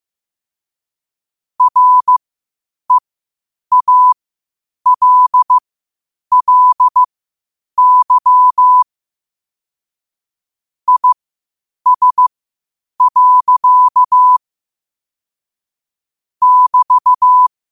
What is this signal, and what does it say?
Morse code, heard as a keyed tone, about 15 wpm, spaced as for 8 wpm, 1000 Hz. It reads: REALLY IS. <BT>
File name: qotdfiles / 8wpm